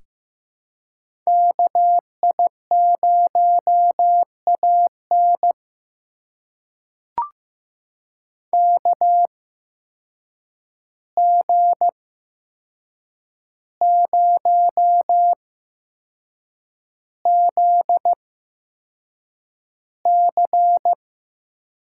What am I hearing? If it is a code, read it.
Morse code, heard as a keyed tone, about 15 wpm, spaced as for 6 wpm, 700 Hz. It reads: KI0AN KG0ZC
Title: USA Call Signs - 5 Characters - 8x Character Spacing - 15wpm